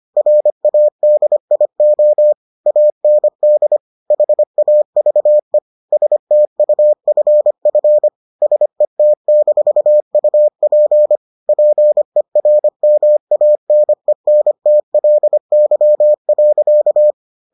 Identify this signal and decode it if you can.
Morse code, heard as a keyed tone, about 25 wpm, 600 Hz. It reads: RADIO AND HAVE STUFF SET-UP PERMANENTLY.